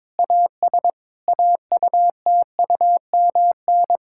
Morse code, heard as a keyed tone, about 22 wpm, 700 Hz. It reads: AS AUTUMN
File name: Difficult Sentences with Hints - 22wpm